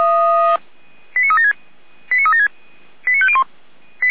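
Five-tone selective call (ZVEI), heard as quick runs of five tones, tones of about 70 ms. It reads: 89376 89376 89501
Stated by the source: ZVEI five-tone call 89150 received by SDR.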